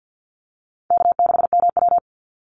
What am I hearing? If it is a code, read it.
Morse code, heard as a keyed tone, about 49 wpm, 700 Hz. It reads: K6MW
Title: NA SOTA Chaser Callsigns 50wpm